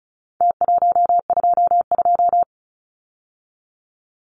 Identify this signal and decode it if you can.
Morse code, heard as a keyed tone, about 34 wpm, 700 Hz. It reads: T122
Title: Numbers - Contest Style - 4-Digit 35wpm